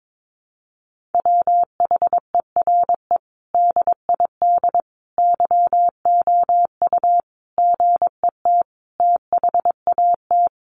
Morse code, heard as a keyed tone, about 22 wpm, 700 Hz. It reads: WHERE DID YOU GET THAT
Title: Sentences from Top 1200 Words 22wpm